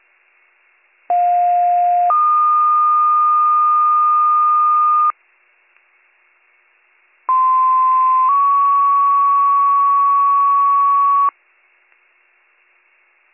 Two-tone page, heard as a one-second tone, then a three-second tone, about 710 then 1220 Hz, about 1040 then 1120 Hz. It is heard from 1.1 to 11.3 s.